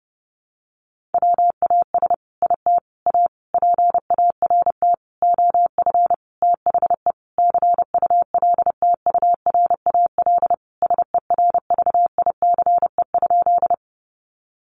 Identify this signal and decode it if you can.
Morse code, heard as a keyed tone, about 30 wpm, 700 Hz. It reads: WAS IT A PART OF THE CULTURAL SERVICE?
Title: Sentences from Top 1500 Words 30wpm